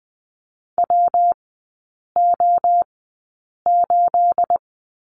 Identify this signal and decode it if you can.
Morse code, heard as a keyed tone, about 20 wpm, spaced as for 8 wpm, 690 Hz. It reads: WO8
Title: 212 Lesson 34e - Number 8 - Three Characters 20wpm